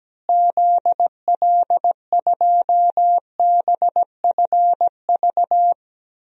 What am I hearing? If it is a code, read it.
Morse code, heard as a keyed tone, about 17 wpm, 700 Hz. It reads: ZL2BFV
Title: DXCC-100 Call Signs - 6 Characters - Repeated 3x - 17wpm